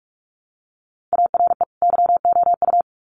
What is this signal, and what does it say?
Morse code, heard as a keyed tone, about 44 wpm, 700 Hz. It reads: ARE YOU
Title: Sets of 2 Words 45wpm